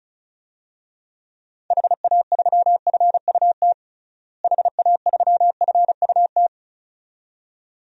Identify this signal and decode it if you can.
Morse code, heard as a keyed tone, about 34 wpm, 700 Hz. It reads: HA3FUT HA3FUT
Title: DXCC-100 Call Signs - 6 Characters - Repeated 2x - 35wpm